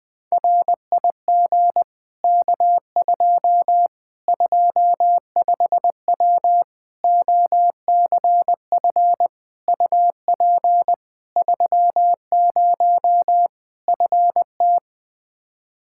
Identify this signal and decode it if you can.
Morse code, heard as a keyed tone, about 21 wpm, 700 Hz. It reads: RIG K2 25W OCF UP 30 FT
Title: Practice QSO - Exchange 2 - Info - 20wpm